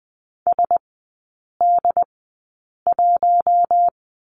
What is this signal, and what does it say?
Morse code, heard as a keyed tone, about 20 wpm, spaced as for 8 wpm, 700 Hz. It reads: SD1
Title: 067 Lesson 12e - Letter d - Three Characters 20wpm